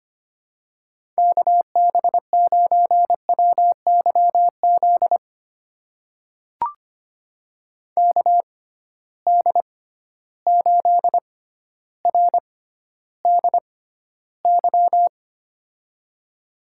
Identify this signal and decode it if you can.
Morse code, heard as a keyed tone, about 25 wpm, spaced as for 13 wpm, 700 Hz. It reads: KB9WYZ KD8RDY
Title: USA Call Signs - 6 Characters - 6x Character Spacing - 25wpm